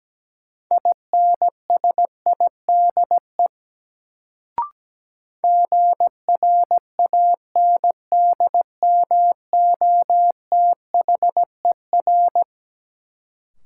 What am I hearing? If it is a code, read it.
Morse code, heard as a keyed tone, about 17 wpm, 700 Hz. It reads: INSIDE GRANDMOTHER
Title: Sentences from Top 1200 Words - New Word Drill Alt - 17wpm